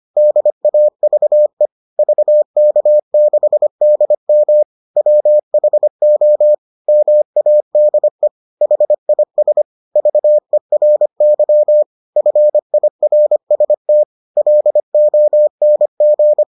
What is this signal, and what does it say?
Morse code, heard as a keyed tone, about 25 wpm, 600 Hz. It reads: DAVE VK6DM WHO MADE HIS VERY FIRST LONG